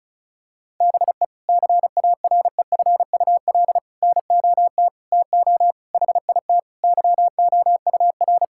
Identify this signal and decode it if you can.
Morse code, heard as a keyed tone, about 36 wpm, 700 Hz. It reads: BE CAREFUL NOT TO HIT YOUR
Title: Sentences from Top 1600 Words 35wpm